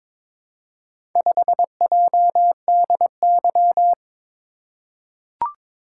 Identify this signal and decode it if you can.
Morse code, heard as a keyed tone, about 22 wpm, 700 Hz. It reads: 5JDY
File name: practice / 22wpm / Four Letter-Number Combinations - 22wpm